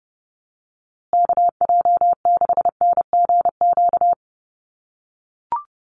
Morse code, heard as a keyed tone, about 30 wpm, 700 Hz. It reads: KJ6NGQ
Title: USA Call Signs Repeated 2x 30wpm